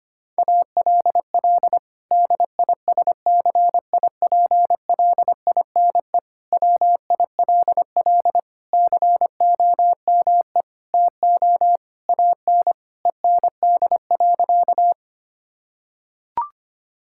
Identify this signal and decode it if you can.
Morse code, heard as a keyed tone, about 25 wpm, 700 Hz. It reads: ALL DISCIPLINE WILL COME TO AN END.
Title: Sentences from Top 1000 Words - Review Part 6 - 25wpm